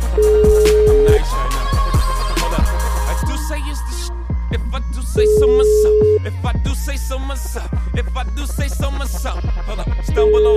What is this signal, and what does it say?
Ringback tone (Europe, 425 Hz).